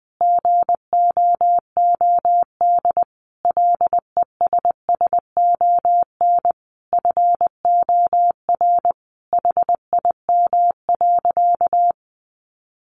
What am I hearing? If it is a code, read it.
Morse code, heard as a keyed tone, about 20 wpm, 700 Hz. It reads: GOOD LESSON FOR HIM.
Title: Sentences from Top 1200 Words 20wpm